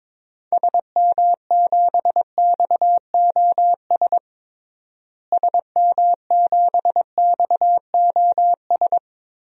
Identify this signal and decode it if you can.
Morse code, heard as a keyed tone, about 22 wpm, 700 Hz. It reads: SM7XOS SM7XOS
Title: DXCC-100 Call Signs - 6 Characters - Repeated 2x - 22wpm